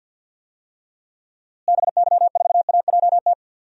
Morse code, heard as a keyed tone, about 51 wpm, 700 Hz. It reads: BY4AJT
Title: DXCC-100 Call Signs - 6 Characters - Repeated 2x - 50wpm